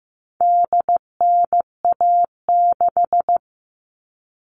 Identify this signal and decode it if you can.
Morse code, heard as a keyed tone, about 15 wpm, 700 Hz. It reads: DNA6
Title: All Cut Numbers - 4-Digit 15wpm